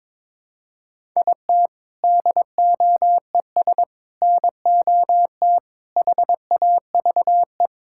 Morse code, heard as a keyed tone, about 22 wpm, 700 Hz. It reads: IT DOES NOT HAVE
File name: Sentences with 7-Word Phrases - 22wpm